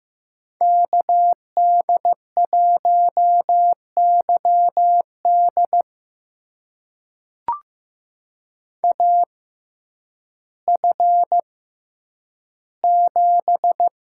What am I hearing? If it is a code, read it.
Morse code, heard as a keyed tone, about 15 wpm, spaced as for 9 wpm, 700 Hz. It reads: KD1YD AF7
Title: USA Call Signs - 5 Characters - 6x Character Spacing - 15wpm